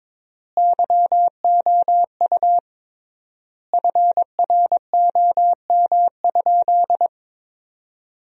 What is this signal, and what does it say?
Morse code, heard as a keyed tone, about 22 wpm, 700 Hz. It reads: YOU FROM?
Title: 3x Word-Spacing - Sentences from Top 200 Words - 22wpm